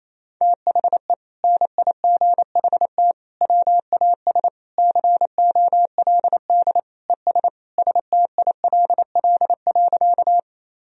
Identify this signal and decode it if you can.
Morse code, heard as a keyed tone, about 28 wpm, 700 Hz. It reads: THE NIGHT WAS COLD ES STILL.
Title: Sentences from Top 400 Words and Abbreviations 28wpm